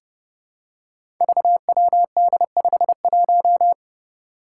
Morse code, heard as a keyed tone, about 30 wpm, 700 Hz. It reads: VWD51
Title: Five Challenging Letter-Number Combinations 30wpm